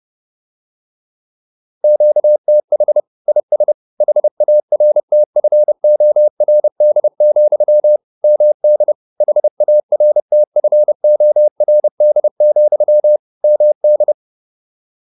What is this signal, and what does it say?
Morse code, heard as a keyed tone, about 30 wpm, 600 Hz. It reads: QTH IS HARTFORD, MD HARTFORD, MD